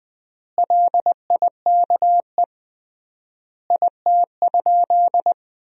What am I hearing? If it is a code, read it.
Morse code, heard as a keyed tone, about 20 wpm, 700 Hz. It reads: LIKE IT?
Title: 3x Word-Spacing - Sentences from Top 300 Words - 20wpm